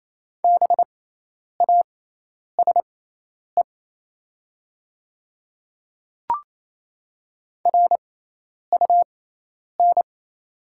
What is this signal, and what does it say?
Morse code, heard as a keyed tone, about 28 wpm, spaced as for 10 wpm, 700 Hz. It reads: BASE RUN
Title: 6x Character Spacing - Top 300 Words - 28wpm